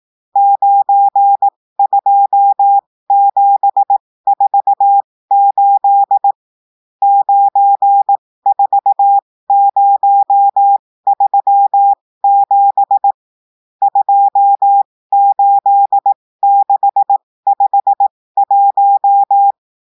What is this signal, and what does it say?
Morse code, heard as a keyed tone, about 18 wpm, spaced as for 15 wpm, 800 Hz. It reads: 92748 94037 28651
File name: cours-du-lundi files / cours-1580-texte-chiffre-15-mpm